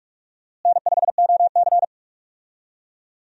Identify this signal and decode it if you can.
Morse code, heard as a keyed tone, about 42 wpm, 700 Hz. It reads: N5OC
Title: USA Call Signs - 4 Characters - Repeated 2x - 45wpm